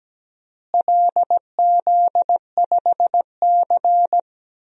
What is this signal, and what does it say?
Morse code, heard as a keyed tone, about 17 wpm, 700 Hz. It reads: LZ5C
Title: DXCC-100 Call Signs - 4 Characters - Repeated 3x - 17wpm